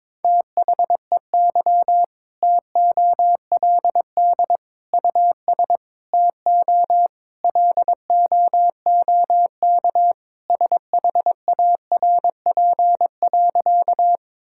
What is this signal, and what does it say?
Morse code, heard as a keyed tone, about 22 wpm, 700 Hz. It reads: THEY TOLD US TO LOOK SHARP.
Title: Sentences from Top 1600 Words 22wpm